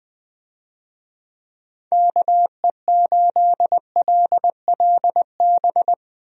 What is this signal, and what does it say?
Morse code, heard as a keyed tone, about 20 wpm, 700 Hz. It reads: KE8LLB
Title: USA Call Signs - 6 Characters - Repeated 2x - 20wpm